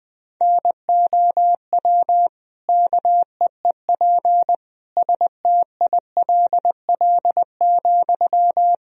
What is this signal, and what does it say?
Morse code, heard as a keyed tone, about 20 wpm, 700 Hz. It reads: NOW KEEP STILL,
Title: Sentences from Top 1000 Words - Review Part 2 - 20wpm